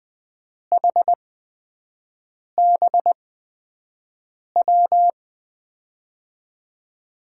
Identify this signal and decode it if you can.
Morse code, heard as a keyed tone, about 20 wpm, spaced as for 5 wpm, 700 Hz. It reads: HBW
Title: USA Call Signs - 5 Characters - 8x Character Spacing - 20wpm